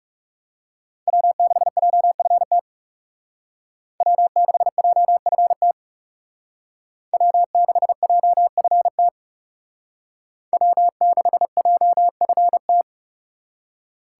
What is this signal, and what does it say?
Morse code, heard as a keyed tone, about 37 wpm, 700 Hz. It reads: W6JFT W6JFT W6JFT W6JFT
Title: Speed Racing - USA 1x3 Call Signs 30wpm